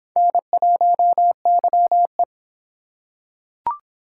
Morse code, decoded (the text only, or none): N1YE